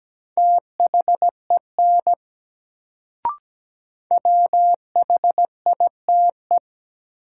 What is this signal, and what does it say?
Morse code, heard as a keyed tone, about 17 wpm, 700 Hz. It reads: THEN WHITE
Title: Sets of 2 Words - New Word Drill - 17wpm